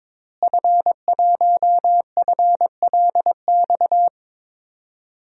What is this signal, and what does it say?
Morse code, heard as a keyed tone, about 21 wpm, 700 Hz. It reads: F1FLX
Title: DXCC-100 Call Signs - 5 Characters - Repeated 3x - 22wpm